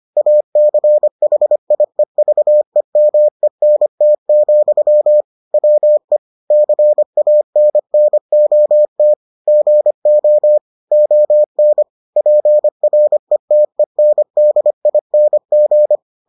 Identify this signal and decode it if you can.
Morse code, heard as a keyed tone, about 25 wpm, 600 Hz. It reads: ACHIEVEMENT, WE CANNOT GO ON PRETENDING